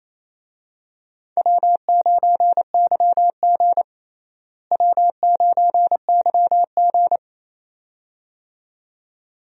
Morse code, heard as a keyed tone, about 28 wpm, 700 Hz. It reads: W9YG W9YG
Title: USA Call Signs - 4 Characters - Repeated 2x - 28wpm